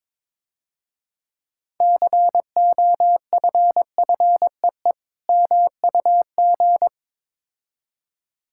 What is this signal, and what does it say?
Morse code, heard as a keyed tone, about 24 wpm, 700 Hz. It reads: COFFEE MUG
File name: Easy Words - Speed-Racing with Hints - 22wpm